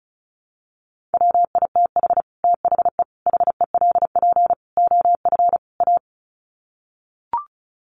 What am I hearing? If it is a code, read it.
Morse code, heard as a keyed tone, about 34 wpm, 700 Hz. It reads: WITH THE HELP OF A
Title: Sets of 5 Words 35wpm